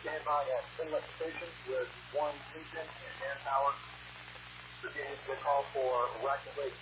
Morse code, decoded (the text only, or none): none